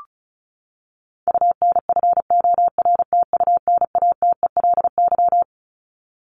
Morse code, heard as a keyed tone, about 34 wpm, 700 Hz. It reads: UNFORTUNATELY
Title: Top 1751-2000 Words 35wpm